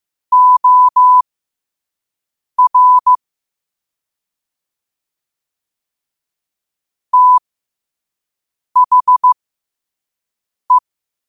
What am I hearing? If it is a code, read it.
Morse code, heard as a keyed tone, about 15 wpm, spaced as for 5 wpm, 1000 Hz. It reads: OR THE
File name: qotdfiles / 5wpm